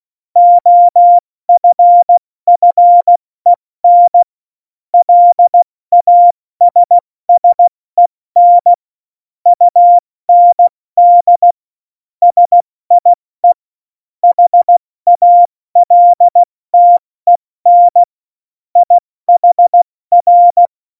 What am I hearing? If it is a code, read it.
Morse code, heard as a keyed tone, about 16 wpm, 700 Hz. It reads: OFFEN LASSEN UND SIE HALTEN IHR
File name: Morsekurs / Witze / Witz 16. Januar 2018, 70 F80